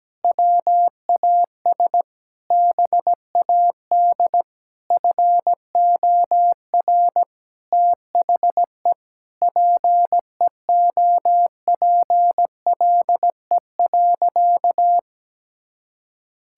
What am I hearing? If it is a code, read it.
Morse code, heard as a keyed tone, about 17 wpm, 700 Hz. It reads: WAS BAD FOR THE PEOPLE.
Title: Sentences from Top 1000 Words - Review Part 5 - 17wpm